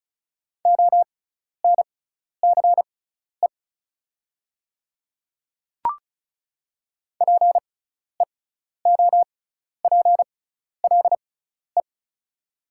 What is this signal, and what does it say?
Morse code, heard as a keyed tone, about 34 wpm, spaced as for 12 wpm, 700 Hz. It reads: ONCE PEOPLE
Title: 6x Character Spacing - Top 300 Words - 35wpm